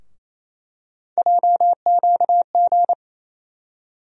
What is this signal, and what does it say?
Morse code, heard as a keyed tone, about 28 wpm, 700 Hz. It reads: JQG
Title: USA Call Sign Postfixes - 3 Letters 28wpm